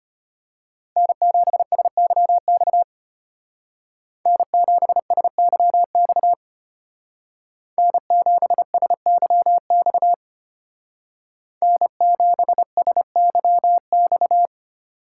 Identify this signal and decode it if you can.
Morse code, heard as a keyed tone, about 30 wpm, 700 Hz. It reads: N7SYX N7SYX N7SYX N7SYX
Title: Speed Racing - USA 1x3 Call Signs 25wpm